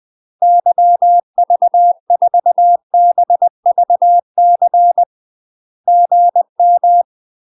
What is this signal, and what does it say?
Morse code, heard as a keyed tone, about 20 wpm, 700 Hz. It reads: YV4BVC GM
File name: UK MASTER QSOs TRAINING/20WPM